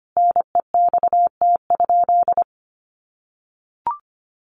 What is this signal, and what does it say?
Morse code, heard as a keyed tone, about 25 wpm, 700 Hz. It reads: NEXT?
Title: Sentences from Top 1200 Words 25wpm